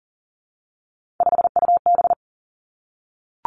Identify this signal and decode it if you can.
Morse code, heard as a keyed tone, about 40 wpm, 700 Hz. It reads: 5UB
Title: Three Challenging Letter-Number Combinations 40wpm